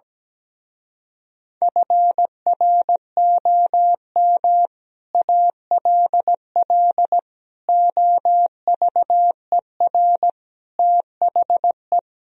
Morse code, heard as a keyed tone, about 17 wpm, 700 Hz. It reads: FROM ALL OVER THE
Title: Speed-Racing - Sets of 5 Words 17wpm